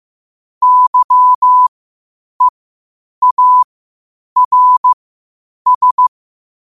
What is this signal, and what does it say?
Morse code, heard as a keyed tone, about 15 wpm, spaced as for 8 wpm, 1000 Hz. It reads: YEARS
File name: qotdfiles / 8wpm